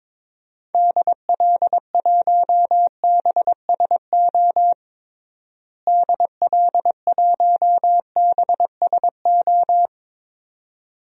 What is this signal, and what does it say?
Morse code, heard as a keyed tone, about 22 wpm, 700 Hz. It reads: DL1BSO DL1BSO
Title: DXCC-100 Call Signs - 6 Characters - Repeated 2x - 22wpm